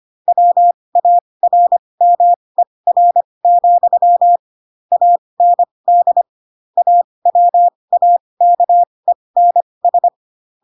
Morse code, heard as a keyed tone, about 25 wpm, spaced as for 20 wpm, 700 Hz. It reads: WARMER, AND AWAKENS
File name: Pinocchio / Pinocchio ch6 15 25 ik1hge